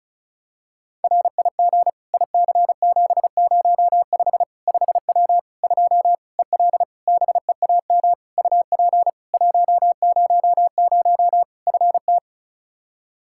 Morse code, heard as a keyed tone, about 36 wpm, 700 Hz. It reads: RIG IC705 5W 2 EL BEAM UP 100 FT
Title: K1USN SST QSO Practice 35wpm